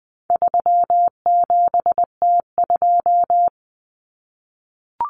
Morse code, heard as a keyed tone, about 20 wpm, 700 Hz. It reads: 37T2
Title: Common Cut Numbers - 4-Digit 20wpm